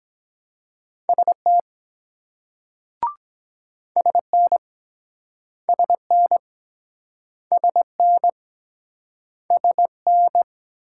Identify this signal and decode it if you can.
Morse code, heard as a keyed tone, about 24 wpm, 700 Hz. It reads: ST SN SN SN SN